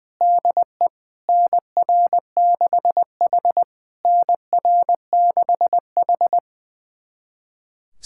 Morse code, heard as a keyed tone, about 20 wpm, 690 Hz. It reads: DE NR6H NR6H
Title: SOTA QSO Practice 20wpm